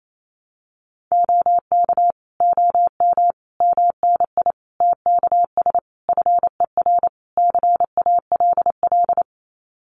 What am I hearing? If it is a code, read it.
Morse code, heard as a keyed tone, about 28 wpm, 700 Hz. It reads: OK OM MNI TKS FER CALL
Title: Common QSO Phrases 28wpm